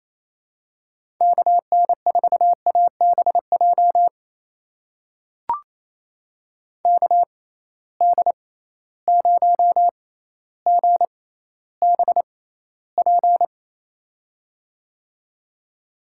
Morse code, heard as a keyed tone, about 28 wpm, spaced as for 14 wpm, 700 Hz. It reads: KN4ABJ KD0GBP
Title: USA Call Signs - 6 Characters - 6x Character Spacing - 28wpm